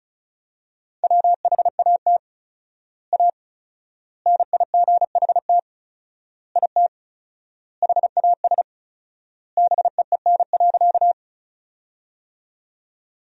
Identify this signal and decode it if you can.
Morse code, heard as a keyed tone, about 34 wpm, 700 Hz. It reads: WHAT A NIGHT IT HAS BEEN.
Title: 4x Word-Spacing - Sentences from Top 300 Words - 35wpm